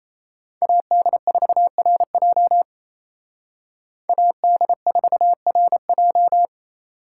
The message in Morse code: AD4RJ AD4RJ

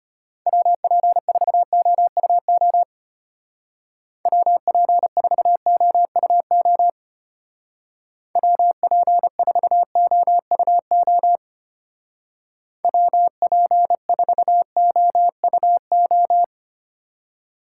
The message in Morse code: WP4OUO WP4OUO WP4OUO WP4OUO

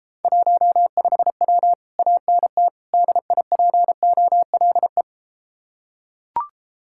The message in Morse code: 15W ANT DIPOLE

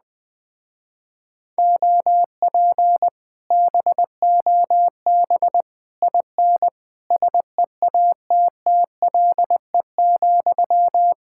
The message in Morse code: OP BOB IN SEATTLE,

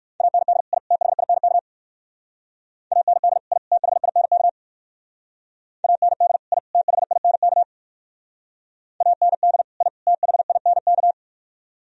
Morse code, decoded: AND I THINK AND I THINK AND I THINK AND I THINK

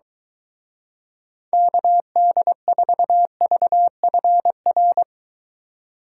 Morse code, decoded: KD4VFR

